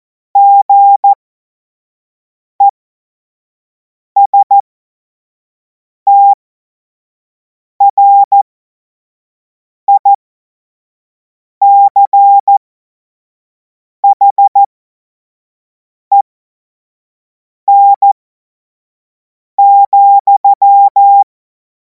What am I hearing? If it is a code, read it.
Morse code, heard as a keyed tone, about 14 wpm, spaced as for 5 wpm, 800 Hz. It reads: GESTRICHEN,
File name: Morsekurs / Witze / Witz 10. Februar 2026 25 F70